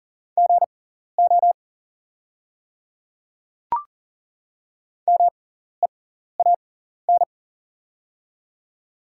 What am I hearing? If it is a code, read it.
Morse code, heard as a keyed tone, about 40 wpm, spaced as for 14 wpm, 690 Hz. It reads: GO MEAN
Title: 6x Character Spacing - Top 500 Words - 40wpm